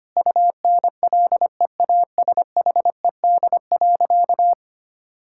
Morse code, heard as a keyed tone, about 25 wpm, 700 Hz. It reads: UNLEASHED.
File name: practice / 25wpm / Difficult Sentences with Hints - 25wpm